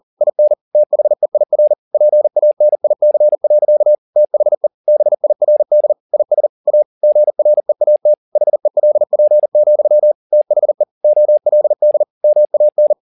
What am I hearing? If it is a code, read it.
Morse code, heard as a keyed tone, about 40 wpm, 600 Hz. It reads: IN THEIR PANIC. THE BIRD IS A GREAT HELP, THE OLD MAN